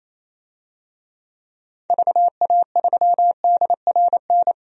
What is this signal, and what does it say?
Morse code, heard as a keyed tone, about 28 wpm, 700 Hz. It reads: VA3DRN
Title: DXCC-100 Call Signs - 6 Characters - 28wpm